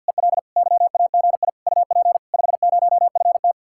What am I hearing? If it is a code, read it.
Morse code, heard as a keyed tone, about 64 wpm, 700 Hz. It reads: EL YAGI UP 50FT